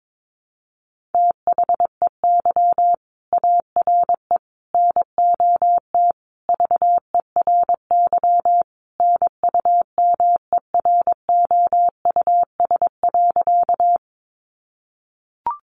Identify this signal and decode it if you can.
Morse code, heard as a keyed tone, about 22 wpm, 700 Hz. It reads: THEY ARE NOT VERY NUMEROUS.
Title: Sentences from Top 1400 Words 22wpm